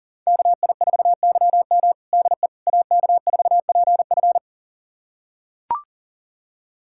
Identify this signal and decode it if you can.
Morse code, heard as a keyed tone, about 40 wpm, 700 Hz. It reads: KI4YM DE AK4PF
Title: USA Callsign DE Callsign - 5 Characters - 40wpm